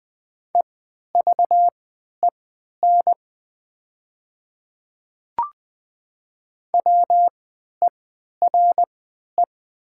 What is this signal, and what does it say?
Morse code, heard as a keyed tone, about 20 wpm, spaced as for 11 wpm, 690 Hz. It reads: EVEN WERE
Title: 3x Character Spacing - Top 200 Words - 20wpm